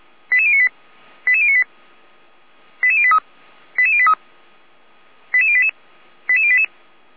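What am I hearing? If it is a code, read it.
Five-tone selective call (ZVEI), heard as quick runs of five tones, tones of about 70 ms. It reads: 88098 88098 88083 88083 88088 88088
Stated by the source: ZVEI five-tone selective call received with an SDR.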